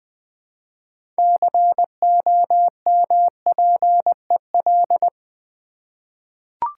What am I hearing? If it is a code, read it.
Morse code, heard as a keyed tone, about 20 wpm, 700 Hz. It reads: COMPEL
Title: 6 Letter Words - First 3 Letters Given - 20wpm